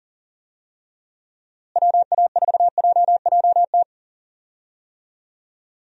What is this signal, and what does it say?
Morse code, heard as a keyed tone, about 37 wpm, 700 Hz. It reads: WA4JJT
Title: USA Call Signs - 6 Characters - Repeated 2x - 40wpm